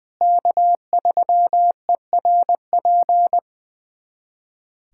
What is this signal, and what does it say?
Morse code, heard as a keyed tone, about 21 wpm, 700 Hz. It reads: K3ERP
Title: USA Callsign DE Callsign 20wpm